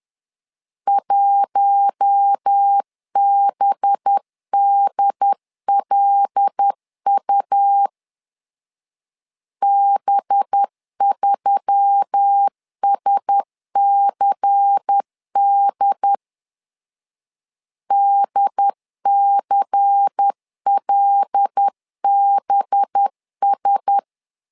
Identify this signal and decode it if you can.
Morse code, heard as a keyed tone, about 10 wpm, 800 Hz. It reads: 1BDLU B3SCD DCLBS
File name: LEZIONI / 5 LBVC / 5 LBVC 10wpm 2349